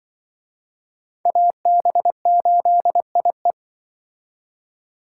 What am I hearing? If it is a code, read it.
Morse code, heard as a keyed tone, about 24 wpm, 700 Hz. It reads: AB8IE
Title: Speed Racing - USA 2x2 Call Signs 20wpm